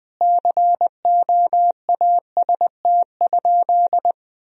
Morse code, heard as a keyed tone, about 20 wpm, 700 Hz. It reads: COAST?
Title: Sentences from Top 1000 Words - Review Part 4 - 20wpm